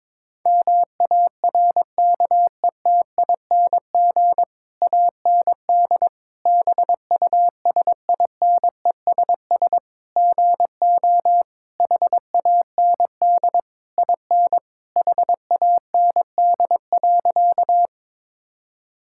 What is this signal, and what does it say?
Morse code, heard as a keyed tone, about 22 wpm, 700 Hz. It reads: MARKETING AND BUSINESS GO HAND IN HAND.
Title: Sentences from Top 1000 Words - Review Part 5 - 22wpm